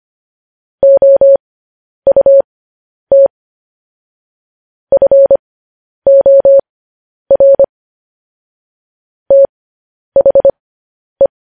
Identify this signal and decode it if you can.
Morse code, heard as a keyed tone, about 25 wpm, spaced as for 10 wpm, 550 Hz. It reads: OUT FOR THE
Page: news-headlines-in-morse-code-at-10-wpm